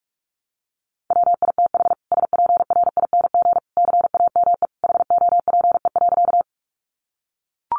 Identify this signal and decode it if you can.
Morse code, heard as a keyed tone, about 44 wpm, 700 Hz. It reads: WITH SPRING CAME HOPE.
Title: Sentences from Top 1200 Words 45wpm